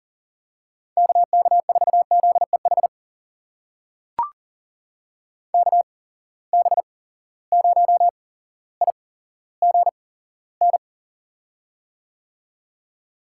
Morse code, heard as a keyed tone, about 38 wpm, spaced as for 17 wpm, 700 Hz. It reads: KK4ZEH KB0IGN
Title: USA Call Signs - 6 Characters - 8x Character Spacing - 40wpm